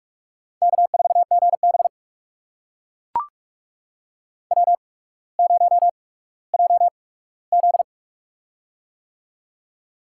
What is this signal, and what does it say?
Morse code, heard as a keyed tone, about 44 wpm, spaced as for 19 wpm, 700 Hz. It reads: K4GB W0JZ
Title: USA Call Signs - 4 Characters - 8x Character Spacing - 45wpm